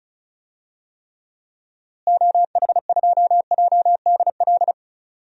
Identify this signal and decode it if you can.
Morse code, heard as a keyed tone, about 36 wpm, 700 Hz. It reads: OH2JDL
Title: DXCC-100 Call Signs - 6 Characters - Repeated 2x - 35wpm